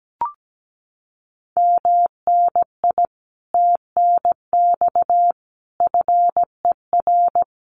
Morse code, heard as a keyed tone, about 17 wpm, 700 Hz. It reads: MNI TNX FER
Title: Common QSO Phrases Encore 17wpm